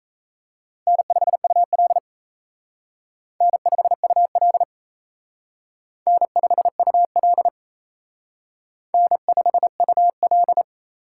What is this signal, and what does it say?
Morse code, heard as a keyed tone, about 36 wpm, 700 Hz. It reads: N5UL N5UL N5UL N5UL